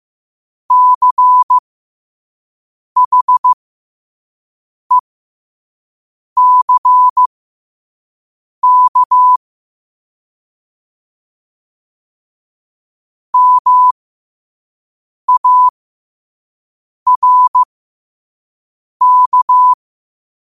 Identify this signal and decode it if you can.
Morse code, heard as a keyed tone, about 15 wpm, spaced as for 5 wpm, 1000 Hz. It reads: CHECK MARK